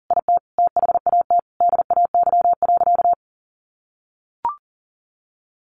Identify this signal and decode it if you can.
Morse code, heard as a keyed tone, about 40 wpm, 700 Hz. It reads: IT THAT DAY.
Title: Sentences from Top 1500 Words 40wpm